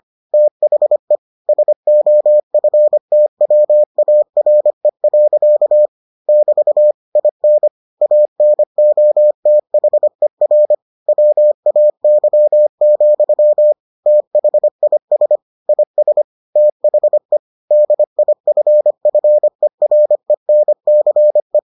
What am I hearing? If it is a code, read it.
Morse code, heard as a keyed tone, about 25 wpm, 600 Hz. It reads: THE SOFTWARE. <BT> IN ANOTHER WAY, THIS IS THE DIFFERENCE